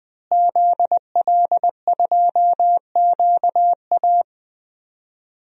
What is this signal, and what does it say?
Morse code, heard as a keyed tone, about 20 wpm, 700 Hz. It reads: ZL2QA
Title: DXCC-100 Call Signs - 5 Characters - Repeated 3x - 20wpm